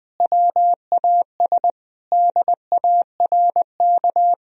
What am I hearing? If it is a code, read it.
Morse code, heard as a keyed tone, about 21 wpm, 700 Hz. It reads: WAS DARK
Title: Sentences from Top 500 Words and Abbreviations 20wpm